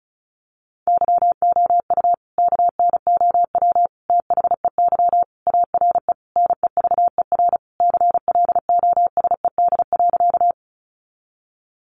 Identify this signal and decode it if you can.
Morse code, heard as a keyed tone, about 34 wpm, 700 Hz. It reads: YOU KNOW THEY ARE NEVER CLOSED.
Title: Sentences from Top 1400 Words 35wpm